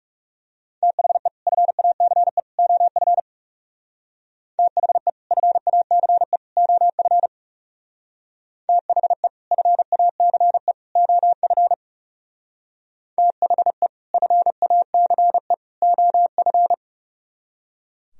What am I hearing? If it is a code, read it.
Morse code, heard as a keyed tone, about 38 wpm, 700 Hz. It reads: THE FACE OF THE FACE OF THE FACE OF THE FACE OF
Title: Speed-Racing - Sets of 3 Words 30wpm